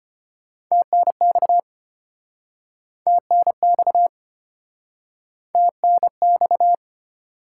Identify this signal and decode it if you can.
Morse code, heard as a keyed tone, about 33 wpm, 700 Hz. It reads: TNX TNX TNX